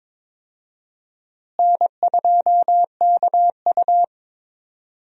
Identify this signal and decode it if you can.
Morse code, heard as a keyed tone, about 22 wpm, 700 Hz. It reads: N2KU
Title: USA Call Signs - 4 Characters - Repeated 2x - 22wpm